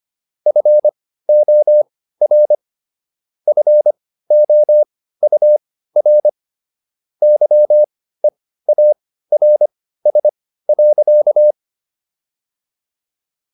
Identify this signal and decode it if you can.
Morse code, heard as a keyed tone, about 25 wpm, spaced as for 15 wpm, 600 Hz. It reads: FOR FOUR YEARS.